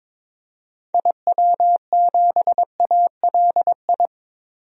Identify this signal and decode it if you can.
Morse code, heard as a keyed tone, about 22 wpm, 700 Hz. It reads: IW7ALI
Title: DXCC-100 Call Signs - 6 Characters - Repeated 3x - 22wpm